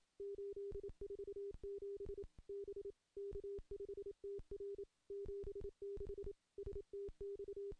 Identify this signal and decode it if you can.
Morse code, heard as a keyed tone, about 27 wpm, 400 Hz. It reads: 847 B K5TR 76 STX
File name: contests / CW-SS-k5tr- -1300